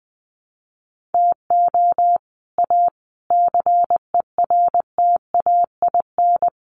Morse code, heard as a keyed tone, about 20 wpm, 700 Hz. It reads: TO A CERTAIN
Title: Sentences from Top 1500 Words 20wpm